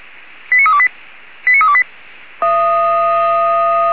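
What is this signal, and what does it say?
Five-tone selective call (ZVEI), heard as quick runs of five tones, tones of about 70 ms. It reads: 89328 89328
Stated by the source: SDR receiver recording of ZVEI five-tone selective call 89324.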